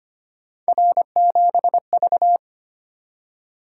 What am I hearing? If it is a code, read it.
Morse code, heard as a keyed tone, about 25 wpm, 700 Hz. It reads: R7V
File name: Three Letter-Number Combinations - 25wpm